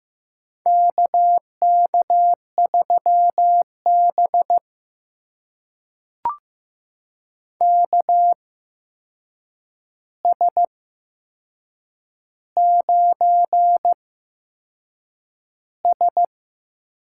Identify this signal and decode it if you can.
Morse code, heard as a keyed tone, about 15 wpm, spaced as for 6 wpm, 700 Hz. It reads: KK3B KS9S
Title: USA Call Signs - 4 Characters - 8x Character Spacing - 15wpm